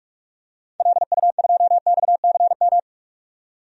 Morse code, heard as a keyed tone, about 47 wpm, 700 Hz. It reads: LU2XCM
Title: DXCC-100 Call Signs - 6 Characters - 45wpm